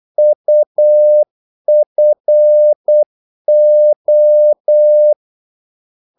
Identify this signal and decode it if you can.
Morse code, heard as a keyed tone, about 8 wpm, 600 Hz. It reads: UFO
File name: Korte ord DK 8wpm